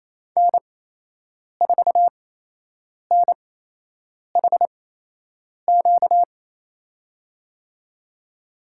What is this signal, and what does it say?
Morse code, heard as a keyed tone, about 28 wpm, spaced as for 8 wpm, 700 Hz. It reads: N4NHQ
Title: USA Call Signs - 8x Character Spacing - 28wpm